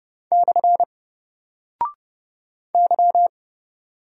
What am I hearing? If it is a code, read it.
Morse code, heard as a keyed tone, about 30 wpm, 700 Hz. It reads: / Y
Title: Single Character (Letter:Number:Punctuation:Prosign) Alt 30wpm